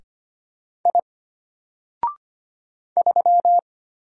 Morse code, heard as a keyed tone, about 25 wpm, 700 Hz. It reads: I 3
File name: Single-Letter-Number-25wpm